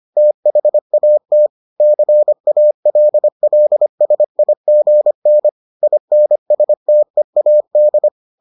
Morse code, heard as a keyed tone, about 25 wpm, 600 Hz. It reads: THAT CALLSIGN INSTEAD